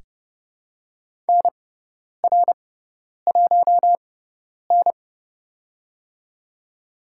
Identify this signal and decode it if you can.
Morse code, heard as a keyed tone, about 31 wpm, spaced as for 10 wpm, 700 Hz. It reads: NR1N
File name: Lesson 17g - Letter u - Call Signs 30wpm